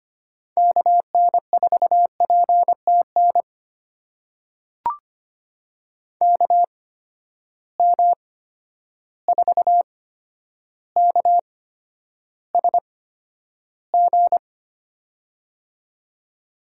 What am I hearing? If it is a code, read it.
Morse code, heard as a keyed tone, about 25 wpm, spaced as for 11 wpm, 700 Hz. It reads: KN4PTN KM4KSG
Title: USA Call Signs - 8x Character Spacing - 25wpm